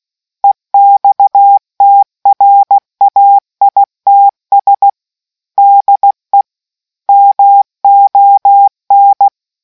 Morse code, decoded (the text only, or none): EXTRAITS DE MON